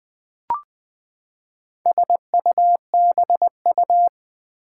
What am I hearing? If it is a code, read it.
Morse code, heard as a keyed tone, about 21 wpm, 700 Hz. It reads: SUBU
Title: Build A Word - 4 Letters At A Time - 8-Letter Words - 20wpm